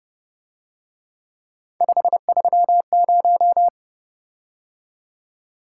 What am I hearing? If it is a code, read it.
Morse code, heard as a keyed tone, about 29 wpm, 700 Hz. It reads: 530
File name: Speed-Racing - Three Letter-Number Combinations - 25wpm